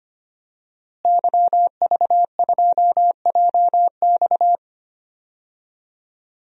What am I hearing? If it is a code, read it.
Morse code, heard as a keyed tone, about 25 wpm, 700 Hz. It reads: YV2JX
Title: DXCC-100 Call Signs - 5 Characters - Repeated 3x - 25wpm